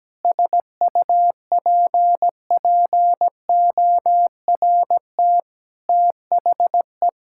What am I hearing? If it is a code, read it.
Morse code, heard as a keyed tone, about 17 wpm, 700 Hz. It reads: SUPPORT THE
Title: Sentences from Top 1000 Words - Review Part 5 - 17wpm